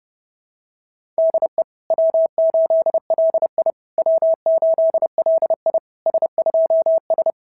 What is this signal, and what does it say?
Morse code, heard as a keyed tone, about 27 wpm, 660 Hz. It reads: DE W8LI W8LI S2S
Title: SOTA QSO Practice 30wpm